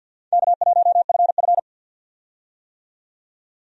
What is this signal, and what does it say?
Morse code, heard as a keyed tone, about 52 wpm, 700 Hz. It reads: K1FF
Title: USA Call Signs - 4 Characters - 50wpm